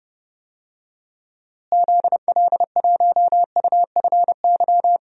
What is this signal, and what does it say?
Morse code, heard as a keyed tone, about 30 wpm, 700 Hz. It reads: ZL1UFY
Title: DXCC-100 Call Signs - 6 Characters - Repeated 2x - 30wpm